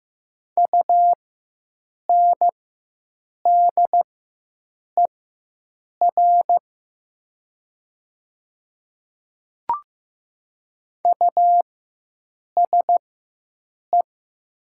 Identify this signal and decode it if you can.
Morse code, heard as a keyed tone, about 15 wpm, spaced as for 7 wpm, 700 Hz. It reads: UNDER USE
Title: 4x Character Spacing - Top 300 Words - 15wpm